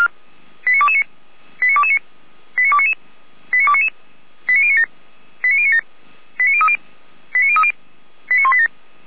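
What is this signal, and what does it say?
Five-tone selective call (ZVEI), heard as quick runs of five tones, tones of about 70 ms. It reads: 89229 89229 89299 89299 89097 89097 89030 89030 89187
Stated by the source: ZVEI five-tone selective call 89277, received by SDR, from the Feuerwehr Orlamünde fire brigade.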